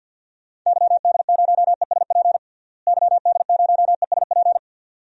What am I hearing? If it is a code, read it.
Morse code, heard as a keyed tone, about 50 wpm, 700 Hz. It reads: YD0ESP YD0ESP
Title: DXCC-100 Call Signs - 6 Characters - Repeated 2x - 50wpm